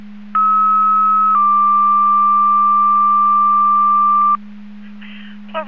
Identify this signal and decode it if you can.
Two-tone page, heard as a one-second tone, then a three-second tone, about 1290 then 1180 Hz, with speech. It is heard from 0.3 to 4.4 s.